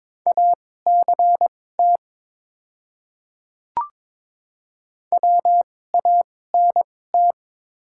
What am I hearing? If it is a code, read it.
Morse code, heard as a keyed tone, about 22 wpm, spaced as for 16 wpm, 700 Hz. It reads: ACT WANT
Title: 2x Character Spacing - Top 300 Words - 22wpm